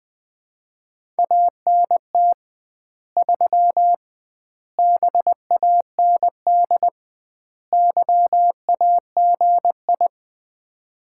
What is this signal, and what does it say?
Morse code, heard as a keyed tone, about 20 wpm, 700 Hz. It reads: ANT 3 BAND YAGI
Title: Common QSO Phrases - 2x Word-Spacing - 20wpm